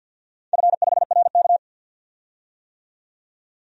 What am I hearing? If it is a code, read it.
Morse code, heard as a keyed tone, about 52 wpm, 700 Hz. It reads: F5RK
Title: DXCC-100 Call Signs - 4 Characters - 50wpm